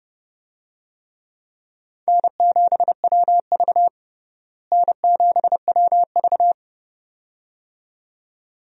Morse code, heard as a keyed tone, about 32 wpm, 700 Hz. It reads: N7WV N7WV
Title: USA Call Signs - 4 Characters - Repeated 2x - 30wpm